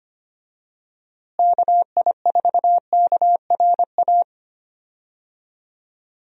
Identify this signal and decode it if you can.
Morse code, heard as a keyed tone, about 25 wpm, 700 Hz. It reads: KI4KRA
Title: USA Call Signs - 6 Characters - Repeated 3x - 25wpm